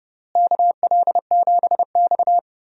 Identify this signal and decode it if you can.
Morse code, heard as a keyed tone, about 32 wpm, 700 Hz. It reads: KL7X
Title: DXCC-100 Call Signs - 4 Characters - Repeated 2x - 30wpm